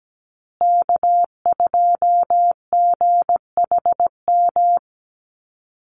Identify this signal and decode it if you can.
Morse code, heard as a keyed tone, about 17 wpm, 700 Hz. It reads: K2GHM
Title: USA Call Signs Repeated 3x 17wpm